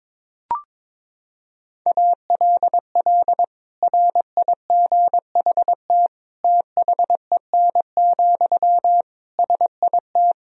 Morse code, heard as a keyed tone, about 22 wpm, 700 Hz. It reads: ALL RIGHT THEN, SIT
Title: Sentences from Top 500 Words and Abbreviations 22wpm